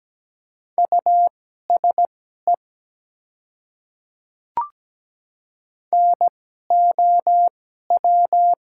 Morse code, heard as a keyed tone, about 17 wpm, spaced as for 12 wpm, 700 Hz. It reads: USE NOW
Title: 2x Character Spacing - Top 100 Words - 17wpm